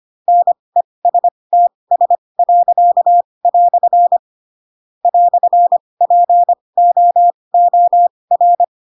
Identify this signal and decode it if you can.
Morse code, heard as a keyed tone, about 25 wpm, spaced as for 20 wpm, 700 Hz. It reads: NESTS." "POOR
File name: Pinocchio / Pinocchio ch4 15 25 ik1hge